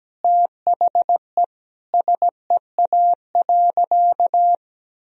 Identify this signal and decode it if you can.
Morse code, heard as a keyed tone, about 17 wpm, 700 Hz. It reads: THE SEA.
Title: Sentences from Top 1000 Words - Review Part 2 - 17wpm